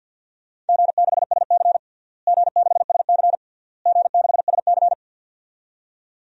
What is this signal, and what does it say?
Morse code, heard as a keyed tone, about 48 wpm, 700 Hz. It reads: G6SC G6SC G6SC
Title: DXCC-100 Call Signs - 4 Characters - Repeated 3x - 50wpm